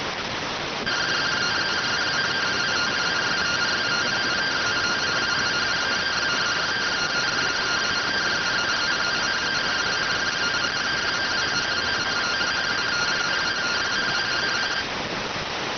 Two-tone teleprinter signal, about 170 Hz shift.